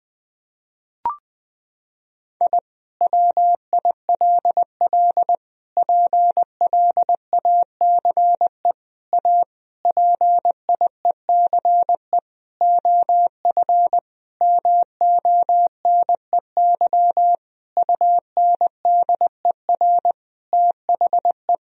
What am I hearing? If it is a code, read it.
Morse code, heard as a keyed tone, about 20 wpm, 700 Hz. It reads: I WILL PLACE A PIECE OF MONEY UNDER THE
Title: Sentences from Top 1000 Words - Review Part 2 - 20wpm